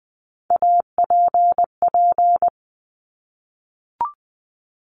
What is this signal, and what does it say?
Morse code, heard as a keyed tone, about 20 wpm, 700 Hz. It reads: APP
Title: Top Three Letter Combinations in Words 20wpm